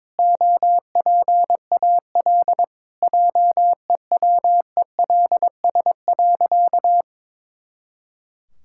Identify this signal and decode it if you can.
Morse code, heard as a keyed tone, about 22 wpm, 700 Hz. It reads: OPAL JEWELS.
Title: Pangram Challenge v2 - 22wpm